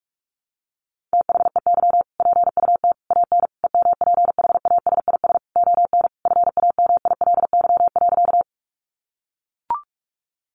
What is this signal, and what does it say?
Morse code, heard as a keyed tone, about 44 wpm, 700 Hz. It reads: THEY PUT AN EMPHASIS ON FAMILY.
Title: Sentences from Top 1400 Words 45wpm